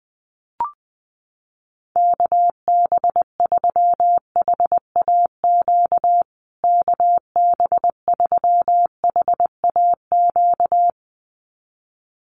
Morse code, heard as a keyed tone, about 20 wpm, 700 Hz. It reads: KB3HAQ KB3HAQ
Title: USA Call Signs Repeated 2x 20wpm